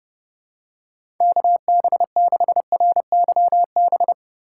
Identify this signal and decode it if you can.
Morse code, heard as a keyed tone, about 32 wpm, 700 Hz. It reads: KB6RYB
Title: USA Call Signs - 6 Characters - 2x Character Spacing - 30wpm